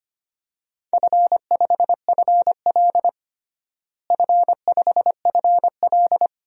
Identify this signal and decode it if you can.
Morse code, heard as a keyed tone, about 25 wpm, 700 Hz. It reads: F5FL F5FL
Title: DXCC-100 Call Signs - 4 Characters - Repeated 2x - 25wpm